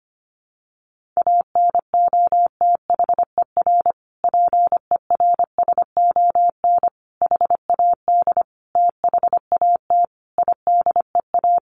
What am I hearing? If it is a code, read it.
Morse code, heard as a keyed tone, about 25 wpm, 700 Hz. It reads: ANOTHER PERSON HAD THAT IDEA
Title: Sentences from Top 1400 Words 25wpm